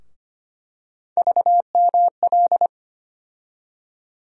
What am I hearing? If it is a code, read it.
Morse code, heard as a keyed tone, about 25 wpm, 700 Hz. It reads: VML
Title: USA Call Sign Postfixes - 3 Letters 25wpm